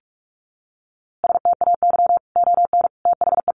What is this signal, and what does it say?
Morse code, heard as a keyed tone, about 44 wpm, 700 Hz. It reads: STAY ON THE
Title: A Sound Of Thunder 45wpm